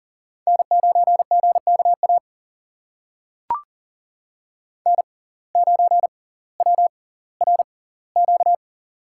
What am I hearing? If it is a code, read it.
Morse code, heard as a keyed tone, about 41 wpm, spaced as for 21 wpm, 690 Hz. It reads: N9GKA N9WRQ
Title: USA Call Signs - 5 Characters - 6x Character Spacing - 40wpm